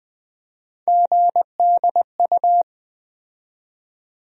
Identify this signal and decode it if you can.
Morse code, heard as a keyed tone, about 20 wpm, 700 Hz. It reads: GDU
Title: USA Call Sign Postfixes - 3 Letters 20wpm